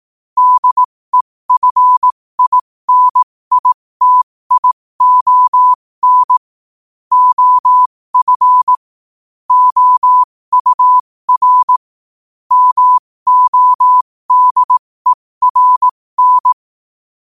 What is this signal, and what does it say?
Morse code, heard as a keyed tone, about 18 wpm, spaced as for 15 wpm, 1000 Hz. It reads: DEFINITION OF OUR MODERN
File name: qotdfiles / 15wpm